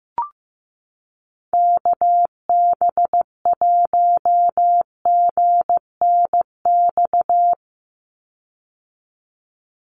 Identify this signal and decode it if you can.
Morse code, heard as a keyed tone, about 15 wpm, 700 Hz. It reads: KB1GNX
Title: USA Call Signs 15wpm